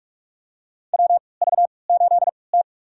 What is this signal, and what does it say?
Morse code, heard as a keyed tone, about 46 wpm, spaced as for 26 wpm, 690 Hz. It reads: WV8T
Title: USA Call Signs - 4 Characters - 3x Character Spacing - 45wpm